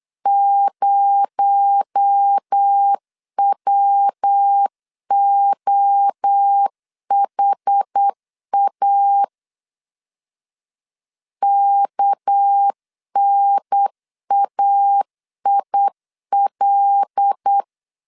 Morse code, heard as a keyed tone, about 8 wpm, 800 Hz. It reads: 0WOHA KNAIL
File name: LEZIONI / 6 PKX0 / 6 PKX0 8wpm 3341